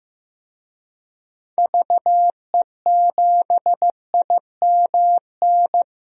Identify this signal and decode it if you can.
Morse code, heard as a keyed tone, about 15 wpm, 700 Hz. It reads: VE7IMN
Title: DXCC-100 Call Signs - 6 Characters - Repeated 2x - 15wpm